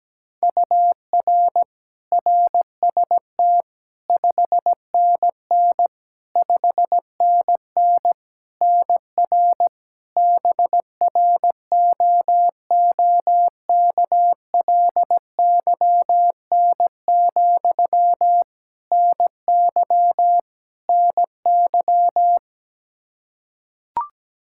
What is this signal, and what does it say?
Morse code, heard as a keyed tone, about 17 wpm, 700 Hz. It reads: UR RST 5NN 5NN NR BROOKLYN, NY NY
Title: K1USN SST QSO Practice 17wpm